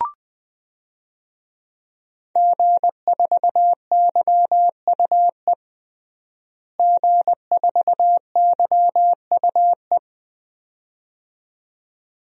Morse code, heard as a keyed tone, about 20 wpm, 700 Hz. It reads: G4YUE G4YUE
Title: DXCC-100 Call Signs - 5 Characters - Repeated 2x - 20wpm